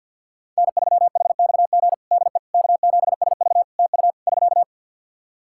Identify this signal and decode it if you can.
Morse code, heard as a keyed tone, about 49 wpm, 700 Hz. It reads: N3HXG DE K7SV TU <SK>
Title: K1USN SST QSO Practice 50wpm